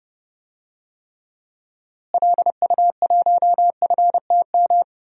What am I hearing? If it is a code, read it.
Morse code, heard as a keyed tone, about 31 wpm, 700 Hz. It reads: LU1FTM
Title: DXCC-100 Call Signs - 6 Characters - Repeated 3x - 30wpm